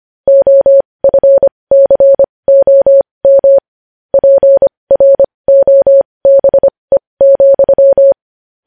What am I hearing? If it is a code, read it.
Morse code, heard as a keyed tone, about 25 wpm, spaced as for 20 wpm, 550 Hz. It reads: OFCOM PROBE,
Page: news headlines in morse code at 20 wpm